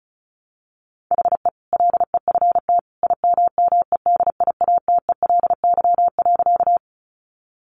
Morse code, heard as a keyed tone, about 34 wpm, 700 Hz. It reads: HE LEFT IMMEDIATELY.
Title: Sentences from Top 1100 Words 35wpm